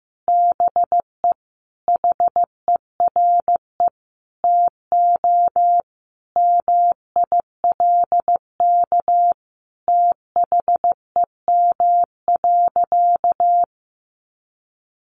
Morse code, heard as a keyed tone, about 15 wpm, 700 Hz. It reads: BE HERE TO MILK THEM.
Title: Sentences from Top 1500 Words 15wpm